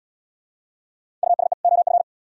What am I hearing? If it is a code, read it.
Morse code, heard as a keyed tone, about 78 wpm, 700 Hz. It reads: USE OF